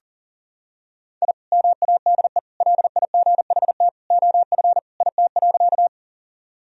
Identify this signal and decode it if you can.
Morse code, heard as a keyed tone, about 40 wpm, 700 Hz. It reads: I MADE LIGHT OF IT.
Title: Sentences from Top 1000 Words - Review Part 1 - 40wpm